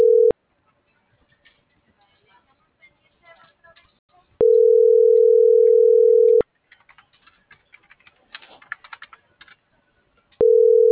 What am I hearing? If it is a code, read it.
Ringback tone (North America), 440 + 480 Hz.
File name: CENTRAL MAYAB / out fernanda.franco